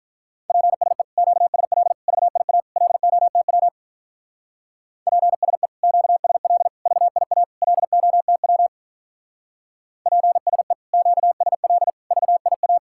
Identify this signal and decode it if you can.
Morse code, heard as a keyed tone, about 44 wpm, 700 Hz. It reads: PSE QSL VIA LOTW PSE QSL VIA LOTW PSE QSL VIA
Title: Speed-Racing Common QSO Phrases - 35wpm